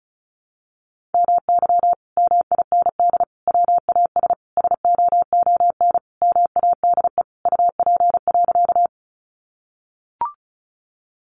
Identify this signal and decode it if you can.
Morse code, heard as a keyed tone, about 34 wpm, 700 Hz. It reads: MY MIND WAS SOON MADE UP.
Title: Sentences from Top 500 Words 35wpm